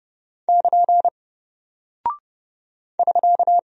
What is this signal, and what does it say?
Morse code, heard as a keyed tone, about 30 wpm, 700 Hz. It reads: <KN> <SK>
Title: Single Character (Letter:Number:Punctuation:Prosign) Alt 30wpm